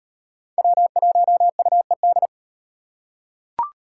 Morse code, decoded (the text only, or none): W1UED